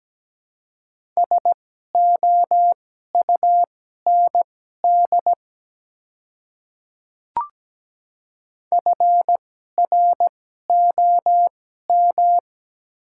Morse code, heard as a keyed tone, about 17 wpm, spaced as for 12 wpm, 700 Hz. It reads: SOUND FROM
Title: 2x Character Spacing - Top 100 Words - 17wpm